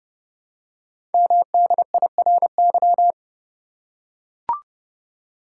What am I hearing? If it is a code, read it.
Morse code, heard as a keyed tone, about 30 wpm, 700 Hz. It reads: MDIRY
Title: Code Groups - 5 Letters - 30wpm